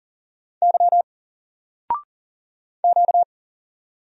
Morse code, heard as a keyed tone, about 40 wpm, 700 Hz. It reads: Y Q
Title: Single Letter-Number Alt 40wpm